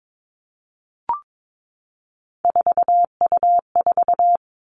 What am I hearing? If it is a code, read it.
Morse code, heard as a keyed tone, about 22 wpm, 700 Hz. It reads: 4U4
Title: Three Challenging Letter-Number Combinations 22wpm